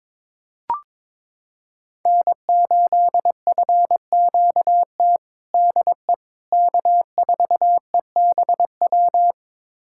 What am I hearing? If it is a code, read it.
Morse code, heard as a keyed tone, about 21 wpm, 700 Hz. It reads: N8FQT DE K4EBW
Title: USA Callsign DE Callsign - 5 Characters - 22wpm